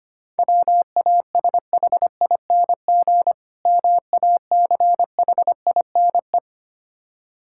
Morse code, heard as a keyed tone, about 26 wpm, 700 Hz. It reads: WASHING MACHINE
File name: practice / 25wpm / Difficult Words - Speed-Racing with Hints - 25wpm